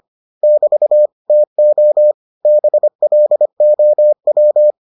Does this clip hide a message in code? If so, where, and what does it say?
Yes: Morse code from 0.4 to 4.7 s, heard as a keyed tone, about 25 wpm, 600 Hz. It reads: <BT> TO BLOW